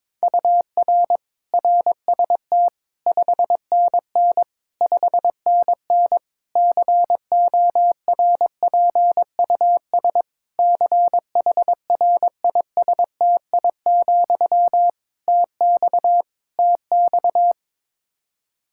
Morse code, heard as a keyed tone, about 22 wpm, 700 Hz. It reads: UR RST 5NN 5NN CORPUS CHRISTI, TX TX
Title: K1USN SST QSO Practice 22wpm